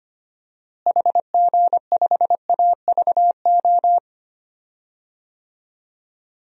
Morse code, heard as a keyed tone, about 25 wpm, 700 Hz. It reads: HG5AVO
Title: DXCC-100 Call Signs - 6 Characters - Repeated 2x - 25wpm